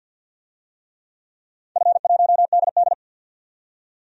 Morse code, heard as a keyed tone, about 45 wpm, 700 Hz. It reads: F1DD
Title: DXCC-100 Call Signs - 4 Characters - 50wpm